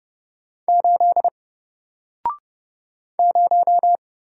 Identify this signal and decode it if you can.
Morse code, heard as a keyed tone, about 30 wpm, 700 Hz. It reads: 8 0
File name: Single-Letter-Number-30wpm